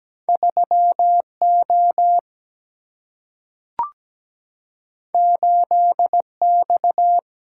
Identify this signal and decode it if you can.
Morse code, heard as a keyed tone, about 17 wpm, 700 Hz. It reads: 3O 8X
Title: Single-Letter Single-Number 17wpm